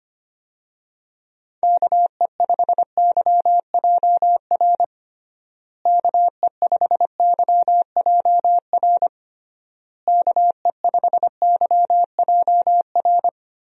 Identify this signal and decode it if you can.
Morse code, heard as a keyed tone, about 25 wpm, 700 Hz. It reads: KE5YJR KE5YJR KE5YJR
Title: USA Call Signs - 6 Characters - Repeated 3x - 25wpm